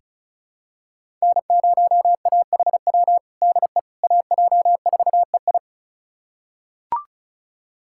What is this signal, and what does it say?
Morse code, heard as a keyed tone, about 36 wpm, 700 Hz. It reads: N0AHW DE AJ4EI
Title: USA Callsign DE Callsign - 5 Characters - 35wpm